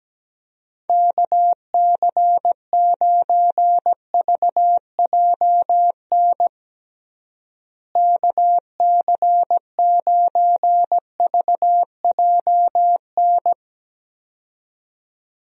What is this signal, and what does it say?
Morse code, heard as a keyed tone, about 17 wpm, 700 Hz. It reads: KC9VJN KC9VJN
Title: USA Call Signs - 6 Characters - Repeated 2x - 17wpm